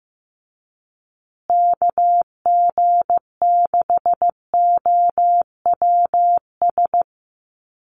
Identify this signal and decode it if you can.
Morse code, heard as a keyed tone, about 15 wpm, 700 Hz. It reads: KG6OWS
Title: USA Call Signs Repeated 3x 15wpm